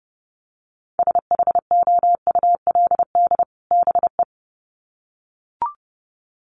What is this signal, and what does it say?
Morse code, heard as a keyed tone, about 30 wpm, 700 Hz. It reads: SHOULD BE
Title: Sets of 2 Words 30wpm